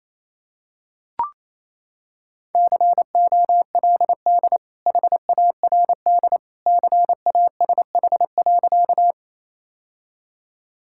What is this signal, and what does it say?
Morse code, heard as a keyed tone, about 28 wpm, 700 Hz. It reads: COLD HARD CASH.
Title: Sentences from Top 1000 Words - Review Part 5 - 28wpm